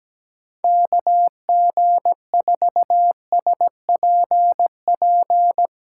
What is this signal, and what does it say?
Morse code, heard as a keyed tone, about 17 wpm, 700 Hz. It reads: KG4SPP
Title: USA Callsign DE Callsign - 6 Characters - 17wpm